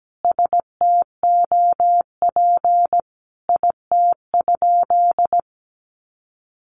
Morse code, heard as a keyed tone, about 17 wpm, 700 Hz. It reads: STOP IT?
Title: Sentences from Top 300 Words 17wpm